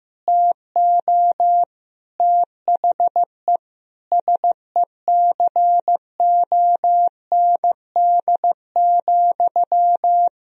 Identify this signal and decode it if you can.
Morse code, heard as a keyed tone, about 15 wpm, 700 Hz. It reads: TO THE SECOND,
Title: Sentences from Top 1000 Words - Review Part 1 - 15wpm